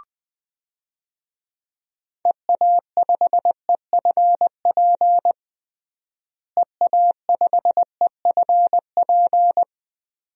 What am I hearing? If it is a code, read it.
Morse code, heard as a keyed tone, about 20 wpm, 700 Hz. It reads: EA5EFP EA5EFP
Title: DXCC-100 Call Signs - 6 Characters - Repeated 2x - 20wpm